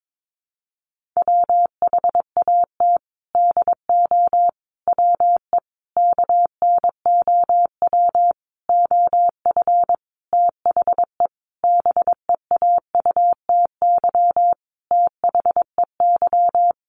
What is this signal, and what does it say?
Morse code, heard as a keyed tone, about 22 wpm, 700 Hz. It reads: WHAT DO WE KNOW OF THE BEAUTY THEY
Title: Sentences from Top 1400 Words 22wpm